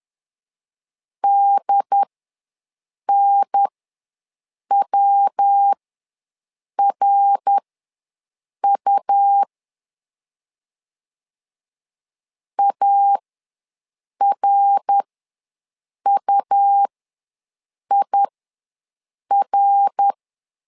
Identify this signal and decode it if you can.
Morse code, heard as a keyed tone, about 11 wpm, spaced as for 6 wpm, 800 Hz. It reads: DNWRU ARUIR
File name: LEZIONI / 3 MWRO / 3 MWR ws 10wpm 2151